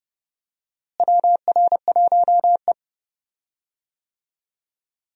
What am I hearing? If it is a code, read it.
Morse code, heard as a keyed tone, about 31 wpm, 700 Hz. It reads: WR1E
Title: USA Call Signs - 4 Characters - Repeated 3x - 30wpm